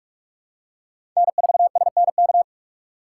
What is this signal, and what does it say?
Morse code, heard as a keyed tone, about 43 wpm, 700 Hz. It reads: N4SNK
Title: USA Call Signs - 5 Characters - 4x Character Spacing - 45wpm